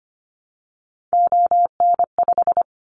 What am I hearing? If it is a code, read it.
Morse code, heard as a keyed tone, about 25 wpm, 700 Hz. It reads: ON5
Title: Callsign Prefixes for Top 100 DXCC - 25wpm